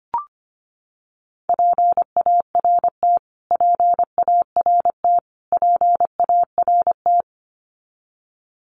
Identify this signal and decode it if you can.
Morse code, heard as a keyed tone, about 25 wpm, 700 Hz. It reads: PART PART PART
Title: Top 500 Words Repeated 25wpm